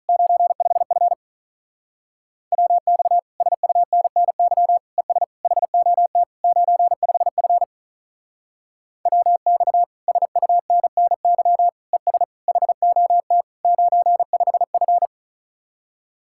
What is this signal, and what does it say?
Morse code, heard as a keyed tone, about 39 wpm, 700 Hz. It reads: 95F WX SUNNY ES HOT 95F WX SUNNY ES HOT 95F
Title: Speed-Racing Common QSO Phrases Encore - 35wpm